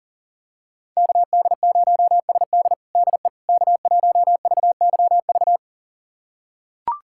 Morse code, heard as a keyed tone, about 40 wpm, 700 Hz. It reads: KD0SD DE K1VYV
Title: USA Callsign DE Callsign 40wpm